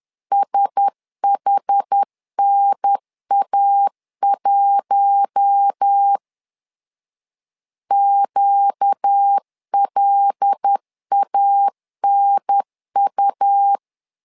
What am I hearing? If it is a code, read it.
Morse code, heard as a keyed tone, about 10 wpm, 800 Hz. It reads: SHNA1 QLANU
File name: LEZIONI / 5 LBVC / 5 L 10wpm 2259